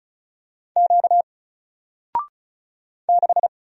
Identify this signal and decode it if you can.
Morse code, heard as a keyed tone, about 35 wpm, 700 Hz. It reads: Q 6
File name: Single-Letter-Number-35wpm